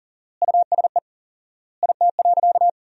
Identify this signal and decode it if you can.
Morse code, heard as a keyed tone, about 40 wpm, 700 Hz. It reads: USE IT.
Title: 4x Word-Spacing - Sentences from Top 500 Words - 40wpm